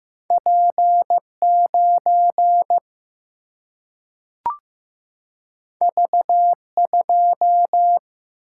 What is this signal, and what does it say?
Morse code, heard as a keyed tone, about 15 wpm, 700 Hz. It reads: P9 V2
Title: Single-Letter Single-Number 15wpm